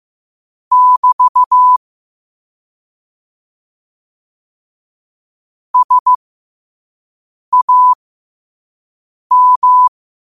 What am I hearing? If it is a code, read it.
Morse code, heard as a keyed tone, about 15 wpm, spaced as for 5 wpm, 1000 Hz. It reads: <BT> SAM
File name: qotdfiles / 5wpm